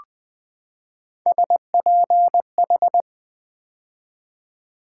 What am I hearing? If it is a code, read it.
Morse code, heard as a keyed tone, about 20 wpm, 700 Hz. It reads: SPH
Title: Build A Word - 3 Letters At A Time - 6-Letter Words - 20wpm